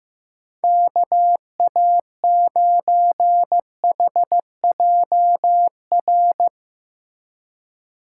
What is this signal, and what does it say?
Morse code, heard as a keyed tone, about 15 wpm, 700 Hz. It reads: KA9HJR
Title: USA Call Signs - 6 Characters - 2x Character Spacing - 15wpm